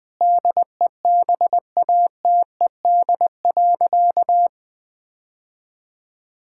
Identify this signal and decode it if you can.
Morse code, heard as a keyed tone, about 20 wpm, 700 Hz. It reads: DEBATED.
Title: Sentences with 7-Word Phrases - Spoken Beginning - 20wpm